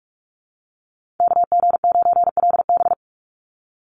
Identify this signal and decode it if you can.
Morse code, heard as a keyed tone, about 46 wpm, 700 Hz. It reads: KG9LB
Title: Call Signs - 45wpm